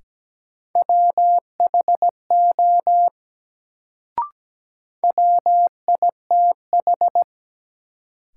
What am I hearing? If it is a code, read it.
Morse code, heard as a keyed tone, about 17 wpm, 700 Hz. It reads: WHO WITH
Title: Sentences from Top 100 Words - New Word Drill - 17wpm